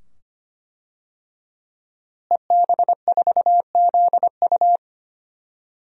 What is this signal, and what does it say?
Morse code, heard as a keyed tone, about 25 wpm, 700 Hz. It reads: EB4ZU
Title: DXCC-100 Call Signs - 5 Characters - Repeated 2x - 25wpm